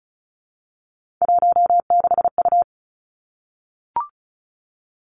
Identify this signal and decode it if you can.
Morse code, heard as a keyed tone, about 36 wpm, 700 Hz. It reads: 16U
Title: Three Challenging Letter-Number Combinations 35wpm